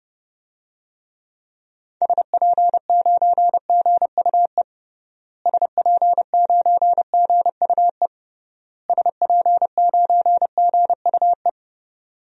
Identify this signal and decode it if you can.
Morse code, heard as a keyed tone, about 30 wpm, 700 Hz. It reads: SP9GUE SP9GUE SP9GUE
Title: DXCC-100 Call Signs - 6 Characters - Repeated 3x - 30wpm